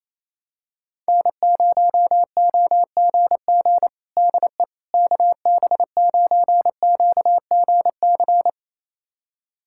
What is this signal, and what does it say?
Morse code, heard as a keyed tone, about 28 wpm, 700 Hz. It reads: N0OGG DE KB9QGC
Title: USA Callsign DE Callsign 28wpm